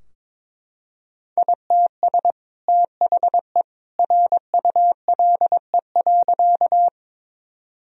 Morse code, heard as a keyed tone, about 22 wpm, 700 Hz. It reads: ITS THE RULE.
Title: Sentences from Top 1000 Words - Review Part 2 - 22wpm